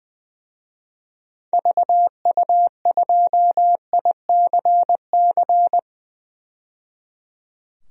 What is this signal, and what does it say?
Morse code, heard as a keyed tone, about 20 wpm, 700 Hz. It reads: VU2ICC
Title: DXCC-100 Call Signs - 6 Characters - 20wpm